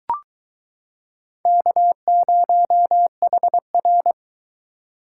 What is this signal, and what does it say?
Morse code, heard as a keyed tone, about 22 wpm, 700 Hz. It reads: K0HR